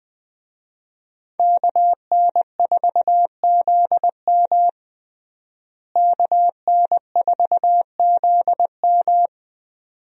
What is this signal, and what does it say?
Morse code, heard as a keyed tone, about 20 wpm, 700 Hz. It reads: KN4ZM KN4ZM
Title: USA Call Signs - 5 Characters - Repeated 2x - 20wpm